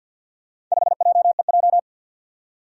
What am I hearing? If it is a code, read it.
Morse code, heard as a keyed tone, about 47 wpm, 700 Hz. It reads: 5JEJ
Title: Four Challenging Letter-Number Combinations 50wpm